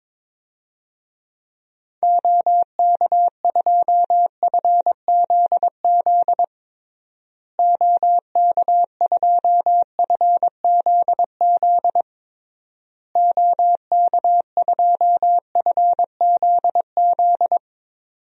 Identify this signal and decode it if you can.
Morse code, heard as a keyed tone, about 21 wpm, 700 Hz. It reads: OK2FZZ OK2FZZ OK2FZZ
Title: DXCC-100 Call Signs - 6 Characters - Repeated 3x - 22wpm